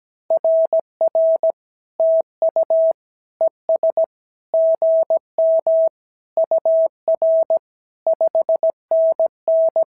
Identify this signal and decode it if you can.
Morse code, heard as a keyed tone, about 16 wpm, 650 Hz. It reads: RR TU ES GM UR 5NN
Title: SOTA QSO Practice 17wpm